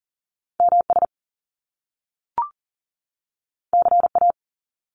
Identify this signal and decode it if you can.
Morse code, heard as a keyed tone, about 42 wpm, 700 Hz. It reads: MS CA
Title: US States 40wpm